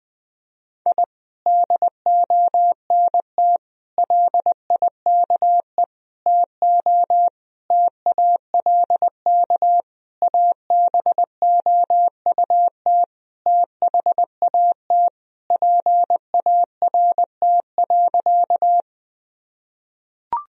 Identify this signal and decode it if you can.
Morse code, heard as a keyed tone, about 21 wpm, 700 Hz. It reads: I DONT LIKE TO TALK ABOUT THAT PART.
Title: Sentences from Top 1000 Words - Review Part 1 - 20wpm